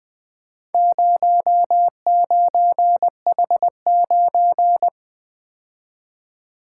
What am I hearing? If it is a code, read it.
Morse code, heard as a keyed tone, about 24 wpm, 700 Hz. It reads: 09H9
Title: Speed-Racing - Four Letter-Number Combinations - 20wpm